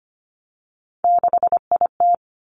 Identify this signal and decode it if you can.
Morse code, heard as a keyed tone, about 25 wpm, 700 Hz. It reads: 6IT
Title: Three Challenging Letter-Number Combinations 25wpm